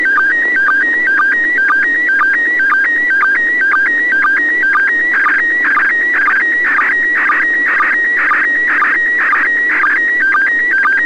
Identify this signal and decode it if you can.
SSTV (slow-scan television), PD 120.
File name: ISS 00h37 UTC 145MHz800 pic05